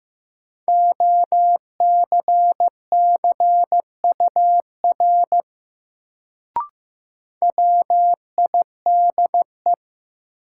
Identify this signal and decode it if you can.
Morse code, heard as a keyed tone, about 15 wpm, 700 Hz. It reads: OCCUR WIDE
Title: Sentences from Top 900 Words - New Word Drill - 15wpm